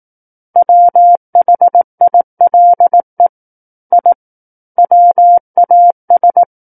Morse code, heard as a keyed tone, about 18 wpm, 700 Hz. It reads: WHILE I WAS
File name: Book One / 09-Chapter 9 - 18 WPM